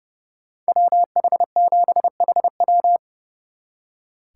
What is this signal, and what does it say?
Morse code, heard as a keyed tone, about 37 wpm, 700 Hz. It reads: WH7HW
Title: Speed Racing - USA 2x2 Call Signs 30wpm